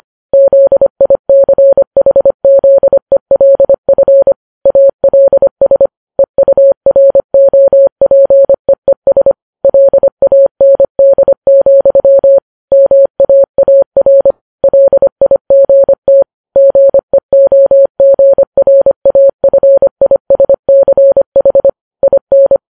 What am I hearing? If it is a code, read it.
Morse code, heard as a keyed tone, about 25 wpm, 550 Hz. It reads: ZICHZELF ALS EUROPEES LAND, MAAR LIGT GEOGRAFISCH IN